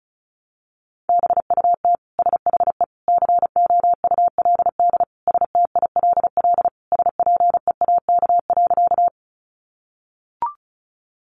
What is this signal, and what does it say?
Morse code, heard as a keyed tone, about 34 wpm, 700 Hz. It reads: BUT SHE COULD STILL SPEAK.
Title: Sentences from Top 1100 Words 35wpm